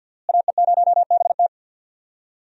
Morse code, heard as a keyed tone, about 46 wpm, 700 Hz. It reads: AE0BT
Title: USA Call Signs - 5 Characters - Repeated 2x - 50wpm